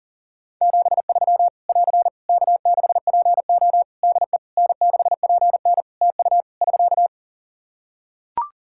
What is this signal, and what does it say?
Morse code, heard as a keyed tone, about 37 wpm, 700 Hz. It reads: 73 <AR> K6PO DE N6PN TU <SK>
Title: K1USN SST QSO Practice 40wpm